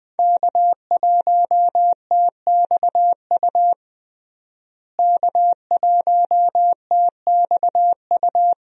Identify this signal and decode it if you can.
Morse code, heard as a keyed tone, about 20 wpm, 700 Hz. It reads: K1TXU K1TXU
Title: USA Call Signs - 5 Characters - Repeated 2x - 20wpm